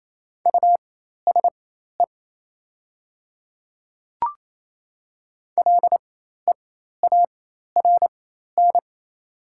Morse code, heard as a keyed tone, about 28 wpm, spaced as for 13 wpm, 700 Hz. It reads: USE LEARN
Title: 4x Character Spacing - Top 300 Words - 28wpm